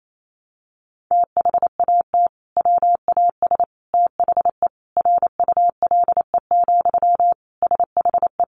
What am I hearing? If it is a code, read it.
Morse code, heard as a keyed tone, about 28 wpm, 700 Hz. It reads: THAT WAS THE RULE, SHE
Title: Sentences from Top 400 Words 28wpm